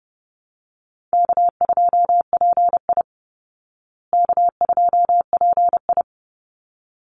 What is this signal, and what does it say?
Morse code, heard as a keyed tone, about 30 wpm, 700 Hz. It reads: K2PI K2PI
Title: Easy Call Signs Repeated 2x - 30wpm